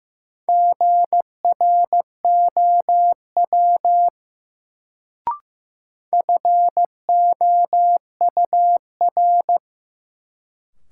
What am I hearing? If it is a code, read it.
Morse code, heard as a keyed tone, about 15 wpm, 700 Hz. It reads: GROW FOUR
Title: Sentences from Top 300 Words - New Word Drill - 15wpm